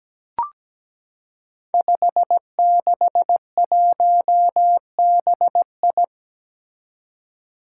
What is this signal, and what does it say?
Morse code, heard as a keyed tone, about 17 wpm, 700 Hz. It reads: 561BI
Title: Five Challenging Letter-Number Combinations 17wpm